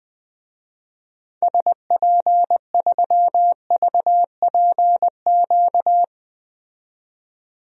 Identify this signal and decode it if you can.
Morse code, heard as a keyed tone, about 20 wpm, 700 Hz. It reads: SP3VPQ
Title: DXCC-100 Call Signs - 6 Characters - 20wpm